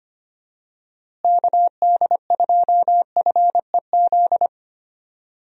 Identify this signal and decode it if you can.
Morse code, heard as a keyed tone, about 25 wpm, 700 Hz. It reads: KD2FEZ
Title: USA Call Signs - 6 Characters - Repeated 2x - 25wpm